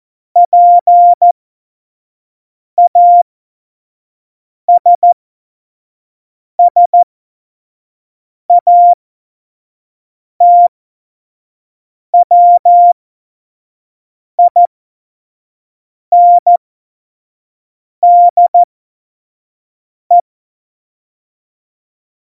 Morse code, heard as a keyed tone, about 14 wpm, spaced as for 5 wpm, 700 Hz. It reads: PASSATWINDE
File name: Morsekurs / Witze / Witz 02. April 2019, 25 F70